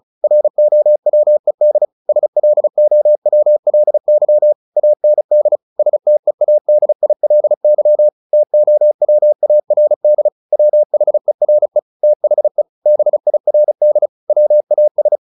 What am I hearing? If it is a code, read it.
Morse code, heard as a keyed tone, about 34 wpm, 600 Hz. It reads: ROWED SLOWLY AND STEADILY TOWARD WHERE THE BIRD WAS
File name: the old man and the sea / the old man and the sea 1 35wpm